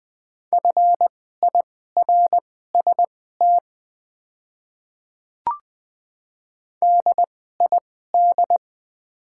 Morse code, heard as a keyed tone, about 21 wpm, spaced as for 15 wpm, 700 Hz. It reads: FIRST DID
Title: 2x Character Spacing - Top 100 Words - 20wpm